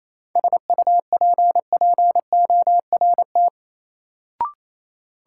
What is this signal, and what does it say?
Morse code, heard as a keyed tone, about 28 wpm, 700 Hz. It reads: SUPPORT
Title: Sentences from Top 800 Words - New Word Drill - 28wpm